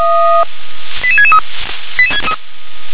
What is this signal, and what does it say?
Five-tone selective call (ZVEI), heard as quick runs of five tones, tones of about 70 ms. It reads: 88663